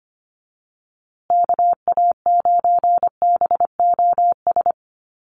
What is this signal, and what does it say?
Morse code, heard as a keyed tone, about 25 wpm, 700 Hz. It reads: KA9BOS
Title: USA Call Signs Repeated 3x 25wpm